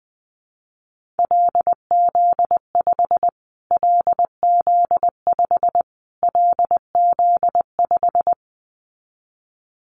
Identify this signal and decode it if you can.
Morse code, heard as a keyed tone, about 20 wpm, 700 Hz. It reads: LZ5 LZ5 LZ5
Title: Callsign Prefixes for Top 25 DXCC - Repeated 3x - 20wpm